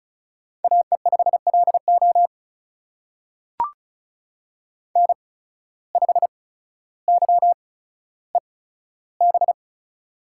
Morse code, heard as a keyed tone, about 36 wpm, spaced as for 15 wpm, 700 Hz. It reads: AE5LO N5YEB
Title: USA Call Signs - 5 Characters - 8x Character Spacing - 35wpm